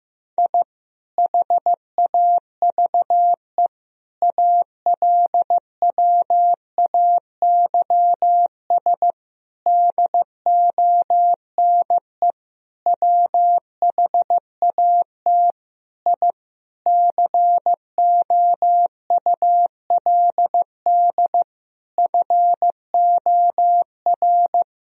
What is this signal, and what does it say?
Morse code, heard as a keyed tone, about 15 wpm, 700 Hz. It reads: I HAVE ALWAYS DONE WHAT I COULD FOR
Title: Sentences from Top 1000 Words - Review Part 2 - 15wpm